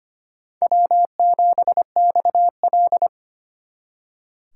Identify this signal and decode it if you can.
Morse code, heard as a keyed tone, about 31 wpm, 700 Hz. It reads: W7XL